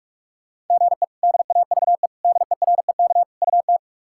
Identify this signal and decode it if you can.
Morse code, heard as a keyed tone, about 41 wpm, 700 Hz. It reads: GE DAVE DEREK UT
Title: K1USN SST QSO Practice - 45wpm